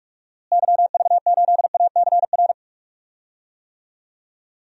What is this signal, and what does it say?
Morse code, heard as a keyed tone, about 42 wpm, 700 Hz. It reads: YV8ACR
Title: DXCC-100 Call Signs - 6 Characters - Repeated 3x - 45wpm